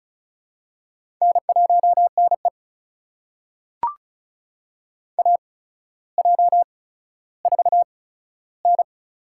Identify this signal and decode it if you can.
Morse code, heard as a keyed tone, about 34 wpm, spaced as for 15 wpm, 700 Hz. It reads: N1NE AJ4N
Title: USA Call Signs - 4 Characters - 8x Character Spacing - 35wpm